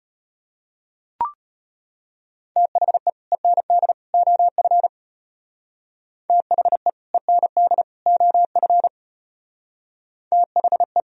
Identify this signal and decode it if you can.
Morse code, heard as a keyed tone, about 33 wpm, 700 Hz. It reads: THE END OF THE END OF THE